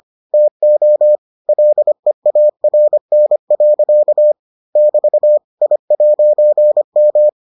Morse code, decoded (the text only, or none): TO LEARN. <BT> I'M